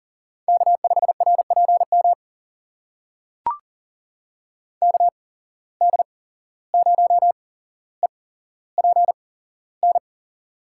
Morse code, heard as a keyed tone, about 40 wpm, spaced as for 16 wpm, 700 Hz. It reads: K5RPM KD0EPN
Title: USA Call Signs - 8x Character Spacing - 40wpm